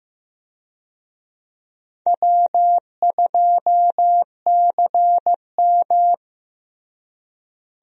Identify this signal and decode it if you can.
Morse code, heard as a keyed tone, about 15 wpm, 700 Hz. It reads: W2CM
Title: USA Call Signs - 4 Characters - Repeated 2x - 15wpm